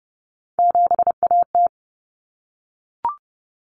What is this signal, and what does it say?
Morse code, heard as a keyed tone, about 30 wpm, 700 Hz. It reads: 7AT
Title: Common Cut Numbers - 3-Digit 30wpm